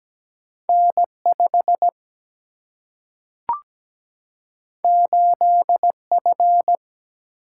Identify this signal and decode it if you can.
Morse code, heard as a keyed tone, about 17 wpm, 700 Hz. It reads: N5 8F
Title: Single-Letter Single-Number 17wpm